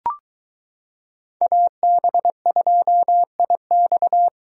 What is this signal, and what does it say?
Morse code, heard as a keyed tone, about 22 wpm, 700 Hz. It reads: AB2IX